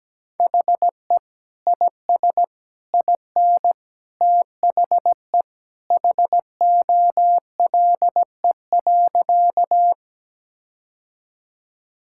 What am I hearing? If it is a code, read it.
Morse code, heard as a keyed tone, about 17 wpm, 700 Hz. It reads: HE IS IN THE HOLE.
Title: Sentences from Top 1600 Words 17wpm